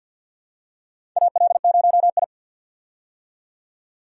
Morse code, heard as a keyed tone, about 48 wpm, 700 Hz. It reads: AL0I
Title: USA Call Signs - 4 Characters - 50wpm